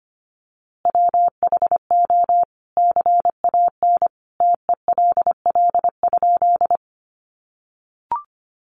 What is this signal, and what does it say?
Morse code, heard as a keyed tone, about 25 wpm, 700 Hz. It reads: WHO CAN TELL?
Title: A Sound Of Thunder 25wpm